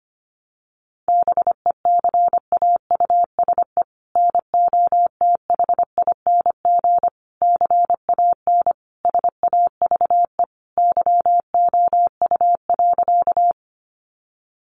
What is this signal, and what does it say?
Morse code, heard as a keyed tone, about 25 wpm, 700 Hz. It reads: BECAUSE NOTHING CAN SAVE YOU.
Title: Sentences from Top 1100 Words 25wpm